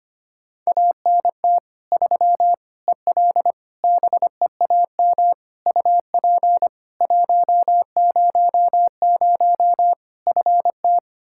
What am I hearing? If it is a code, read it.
Morse code, heard as a keyed tone, about 25 wpm, 700 Hz. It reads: ANT 3 EL BEAM UP 100 FT
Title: K1USN SST QSO Practice 25wpm